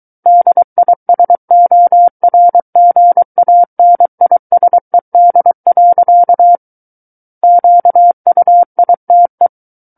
Morse code, heard as a keyed tone, about 23 wpm, 700 Hz. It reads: DISORGANISED. QUITE
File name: Book One / 14-Chapter 14 - 23 WPM